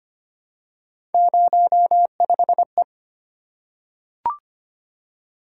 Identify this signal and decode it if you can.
Morse code, heard as a keyed tone, about 25 wpm, 700 Hz. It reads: 05E
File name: Three Letter-Number Combinations - 25wpm